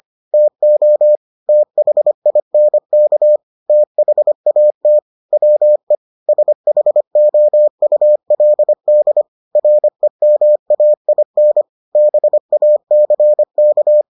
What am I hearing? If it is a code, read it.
Morse code, heard as a keyed tone, about 25 wpm, 600 Hz. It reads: TO THINK THAT WE SHOULD REMAIN BACK